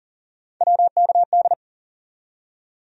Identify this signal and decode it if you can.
Morse code, heard as a keyed tone, about 54 wpm, spaced as for 47 wpm, 700 Hz. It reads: WKD
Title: Speed-Racing Abbreviations - 40wpm